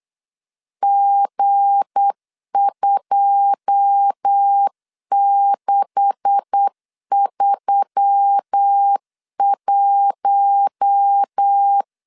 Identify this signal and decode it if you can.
Morse code, heard as a keyed tone, about 8 wpm, 800 Hz. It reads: G2631
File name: CW / DET / 1 9 1500C S C 8wpm 5653